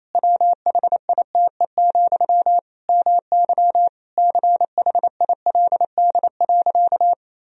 Morse code, heard as a keyed tone, about 28 wpm, 700 Hz. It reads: WHITE, MY CHILD.
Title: Sentences from Top 1000 Words - Review Part 1 - 28wpm